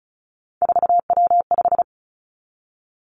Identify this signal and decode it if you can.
Morse code, heard as a keyed tone, about 36 wpm, 700 Hz. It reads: 4W5
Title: Three Challenging Letter-Number Combinations 35wpm